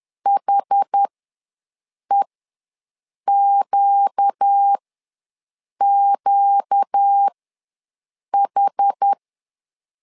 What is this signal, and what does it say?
Morse code, heard as a keyed tone, about 10 wpm, spaced as for 6 wpm, 800 Hz. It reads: HEQQH
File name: LEZIONI / 4 Q ws 10wpm 2306